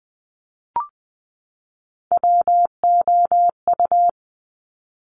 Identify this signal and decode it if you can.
Morse code, heard as a keyed tone, about 20 wpm, 700 Hz. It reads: WOU
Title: Top Three Letter Combinations in Words 20wpm